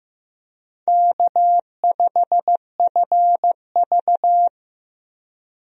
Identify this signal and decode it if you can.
Morse code, heard as a keyed tone, about 16 wpm, 700 Hz. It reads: K5FV
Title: USA Call Signs - 4 Characters - Repeated 3x - 15wpm